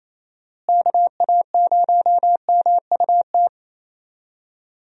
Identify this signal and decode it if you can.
Morse code, heard as a keyed tone, about 27 wpm, 700 Hz. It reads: KA0MUT
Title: USA Call Signs - 6 Characters - Repeated 3x - 28wpm